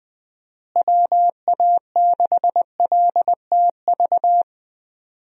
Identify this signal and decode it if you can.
Morse code, heard as a keyed tone, about 20 wpm, 700 Hz. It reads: WA6LTV
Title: USA Call Signs - 6 Characters - Repeated 3x - 20wpm